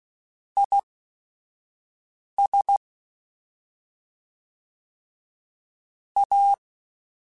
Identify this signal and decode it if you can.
Morse code, heard as a keyed tone, about 16 wpm, spaced as for 5 wpm, 800 Hz. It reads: IS A
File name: Weather Sets / Weather Sets-mp3s / Weather 4 10 @5